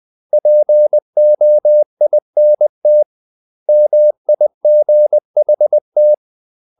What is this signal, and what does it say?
Morse code, heard as a keyed tone, about 20 wpm, 600 Hz. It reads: POINT MIGHT
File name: CW / Practice / 500MostCommonEnglishWords20